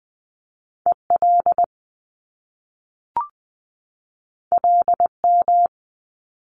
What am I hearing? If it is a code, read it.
Morse code, heard as a keyed tone, about 20 wpm, 700 Hz. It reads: EL LM
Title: Top Two Letter Combinations in Words 20wpm